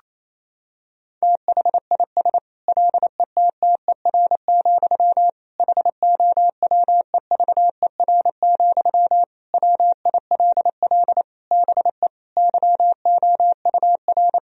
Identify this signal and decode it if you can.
Morse code, heard as a keyed tone, about 28 wpm, 700 Hz. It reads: THIS LETTER, HOWEVER, WILL BE YOUR
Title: Sentences from Top 1000 Words - Review Part 4 - 28wpm